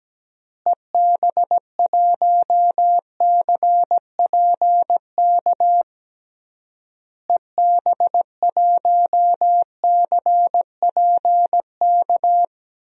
Morse code, heard as a keyed tone, about 17 wpm, 700 Hz. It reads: EB1CPK EB1CPK
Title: DXCC-100 Call Signs - 6 Characters - Repeated 3x - 17wpm